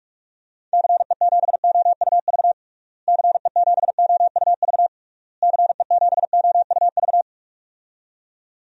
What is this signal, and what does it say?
Morse code, heard as a keyed tone, about 46 wpm, 700 Hz. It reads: CE7OUV CE7OUV CE7OUV
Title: DXCC-100 Call Signs - 6 Characters - Repeated 3x - 45wpm